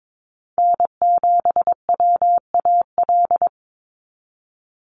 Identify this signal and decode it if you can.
Morse code, heard as a keyed tone, about 22 wpm, 700 Hz. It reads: N7WAL
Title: USA Call Signs 22wpm